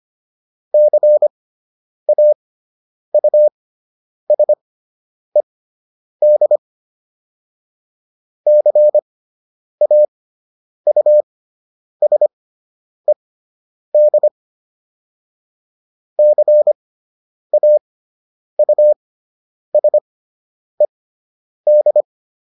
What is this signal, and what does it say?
Morse code, heard as a keyed tone, about 25 wpm, spaced as for 9 wpm, 600 Hz. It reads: CAUSED CAUSED CAUSED